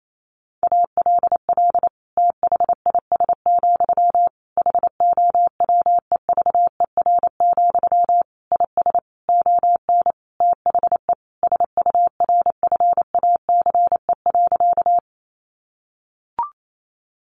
Morse code, heard as a keyed tone, about 28 wpm, 700 Hz. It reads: ALL THIS, HOWEVER, IS ON THE SURFACE.
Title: Sentences from Top 600 Words 28wpm